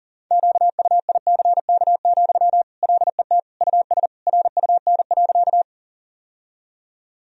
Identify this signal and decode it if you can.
Morse code, heard as a keyed tone, about 40 wpm, 700 Hz. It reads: QUICK, LET US RUN.
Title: Sentences from Top 1000 Words - Review Part 1 - 40wpm